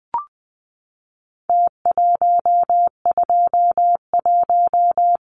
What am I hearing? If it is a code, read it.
Morse code, heard as a keyed tone, about 20 wpm, 700 Hz. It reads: T121
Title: Numbers - Contest Style - 4-Digit 20wpm